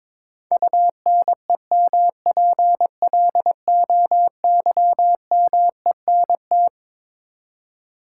Morse code, heard as a keyed tone, about 22 wpm, 700 Hz. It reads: UNEMPLOYMENT
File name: Twelve Letter Words - 22wpm